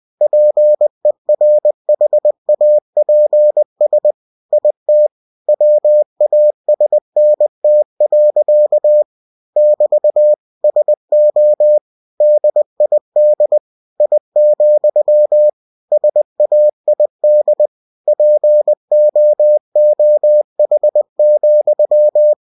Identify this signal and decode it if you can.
Morse code, heard as a keyed tone, about 20 wpm, 600 Hz. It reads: PERHAPS IT WASNT. <BT> SO DID I, SAID POOH,
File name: cw-books / winnie-the-pooh / winnie-20wpm-page-0033